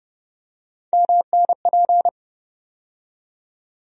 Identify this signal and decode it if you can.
Morse code, heard as a keyed tone, about 31 wpm, 700 Hz. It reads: MNP
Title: USA Call Sign Postfixes - 3 Letters 30wpm